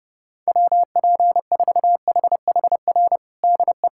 Morse code, heard as a keyed tone, about 32 wpm, 700 Hz. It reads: WP4HHR DE
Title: USA Callsign DE Callsign 30wpm